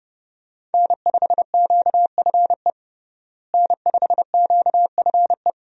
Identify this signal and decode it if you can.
Morse code, heard as a keyed tone, about 32 wpm, 700 Hz. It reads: N5QFE N5QFE
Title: USA Call Signs - 5 Characters - Repeated 3x - 30wpm